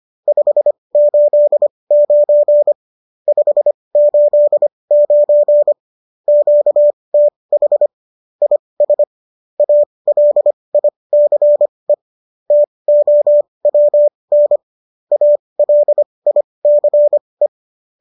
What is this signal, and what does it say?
Morse code, heard as a keyed tone, about 25 wpm, spaced as for 20 wpm, 590 Hz. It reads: 589 589 QTH IS ALICE TOWN ALICE